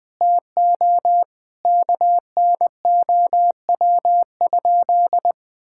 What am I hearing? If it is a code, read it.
Morse code, heard as a keyed tone, about 21 wpm, 700 Hz. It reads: TO KNOW?
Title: Sentences from Top 1600 Words 20wpm